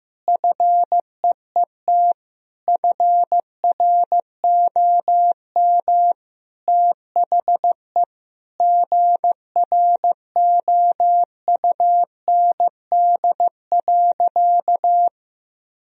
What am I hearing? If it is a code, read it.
Morse code, heard as a keyed tone, about 15 wpm, 700 Hz. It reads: FEET FROM THE GROUND.
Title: Sentences from Top 1000 Words - Review Part 2 - 15wpm